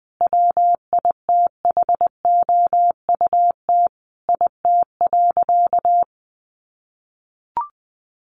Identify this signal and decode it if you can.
Morse code, heard as a keyed tone, about 19 wpm, 700 Hz. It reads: WITHOUT IT.
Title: Sentences from Top 600 Words 20wpm